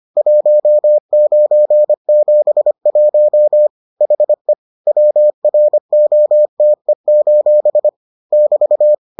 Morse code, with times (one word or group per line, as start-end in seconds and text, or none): 0.2-3.7 1971
4.0-4.5 HE
4.9-7.9 WROTE:
8.3-8.9 <BT>